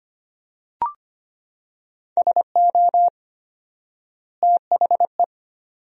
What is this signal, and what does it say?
Morse code, heard as a keyed tone, about 25 wpm, 700 Hz. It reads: SO THE
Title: 4x Word-Spacing - Sentences from Top 400 Words - 25wpm